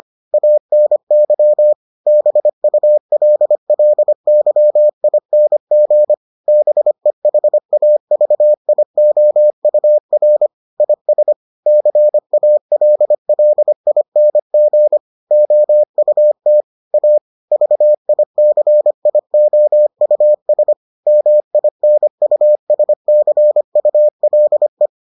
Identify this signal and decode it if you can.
Morse code, heard as a keyed tone, about 25 wpm, 600 Hz. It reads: ANY BULLYING BEHAVIOUR IS CALLING OUT A VICIOUS MINUSCULE